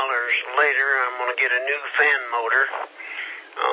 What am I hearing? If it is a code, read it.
Emergency Alert System (SAME) tones, heard as harsh buzzing bursts, with speech. It is heard from 0.6 to 3.5 s.